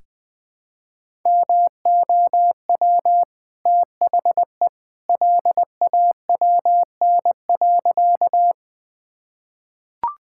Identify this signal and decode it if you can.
Morse code, heard as a keyed tone, about 20 wpm, 700 Hz. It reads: MOW THE LAWN.
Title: Easy Sentences with Hints - 20wpm